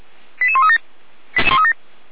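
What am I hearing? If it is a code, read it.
Five-tone selective call (ZVEI), heard as quick runs of five tones, tones of about 80 ms. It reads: 88137
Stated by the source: ZVEI five-tone call 88190 received by SDR.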